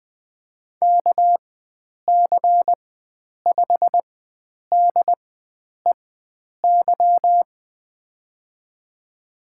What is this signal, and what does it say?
Morse code, heard as a keyed tone, about 21 wpm, spaced as for 9 wpm, 700 Hz. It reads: KC5DEY
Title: USA Call Signs - 6 Characters - 4x Character Spacing - 20wpm